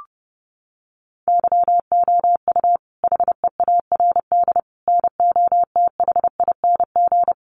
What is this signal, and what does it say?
Morse code, heard as a keyed tone, about 30 wpm, 700 Hz. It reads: YOU HEARD NOTHING
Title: Sentences from Top 600 Words 30wpm